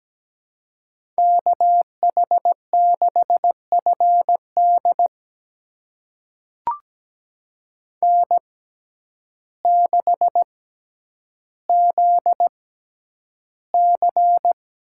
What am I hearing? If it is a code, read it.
Morse code, heard as a keyed tone, about 17 wpm, spaced as for 9 wpm, 700 Hz. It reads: KH6FD N6ZC
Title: USA Call Signs - 5 Characters - 6x Character Spacing - 17wpm